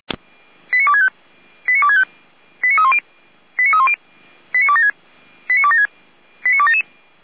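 Five-tone selective call (ZVEI), heard as quick runs of five tones, tones of about 70 ms. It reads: 89265 89265 89310 89310 89276 89276 89277
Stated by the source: ZVEI five-tone call received by SDR, from feuerwehr-orlamuende.org fire brigade audio server.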